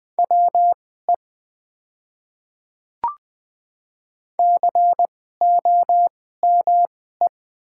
Morse code, heard as a keyed tone, about 20 wpm, spaced as for 14 wpm, 700 Hz. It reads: WE COME
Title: 2x Character Spacing - Top 200 Words - 20wpm